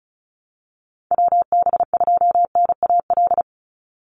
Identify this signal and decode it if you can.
Morse code, heard as a keyed tone, about 34 wpm, 700 Hz. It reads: WB2NAL
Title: USA Call Signs 35wpm